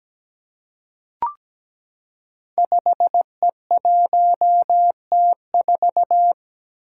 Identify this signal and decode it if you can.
Morse code, heard as a keyed tone, about 17 wpm, 700 Hz. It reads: 5E1T4
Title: Five Challenging Letter-Number Combinations 17wpm